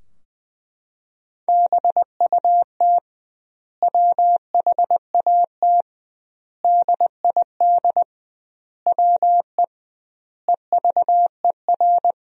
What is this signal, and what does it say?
Morse code, heard as a keyed tone, about 20 wpm, 700 Hz. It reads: BUT WHAT DID WE EVER
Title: 2x Word-Spacing - Sentences from Top 400 Words - 20wpm